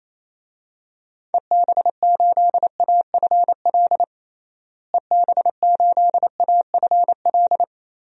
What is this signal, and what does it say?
Morse code, heard as a keyed tone, about 28 wpm, 700 Hz. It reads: EB8AFL EB8AFL
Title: DXCC-100 Call Signs - 6 Characters - Repeated 2x - 28wpm